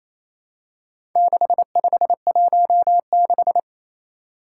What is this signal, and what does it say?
Morse code, heard as a keyed tone, about 27 wpm, 700 Hz. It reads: 6516
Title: Four Challenging Letter-Number Combinations 28wpm